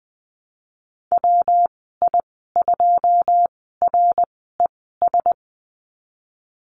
Morse code, heard as a keyed tone, about 20 wpm, spaced as for 14 wpm, 700 Hz. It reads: WI2RES
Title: 193 Lesson 31g - Letter v - Call Signs 20wpm